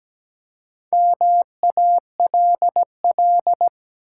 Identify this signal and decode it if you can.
Morse code, heard as a keyed tone, about 17 wpm, 700 Hz. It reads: MALL
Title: Four Letter Words - 17wpm